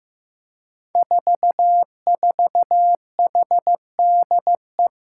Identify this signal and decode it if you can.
Morse code, heard as a keyed tone, about 15 wpm, 700 Hz. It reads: 44HDE
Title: Five Challenging Letter-Number Combinations 15wpm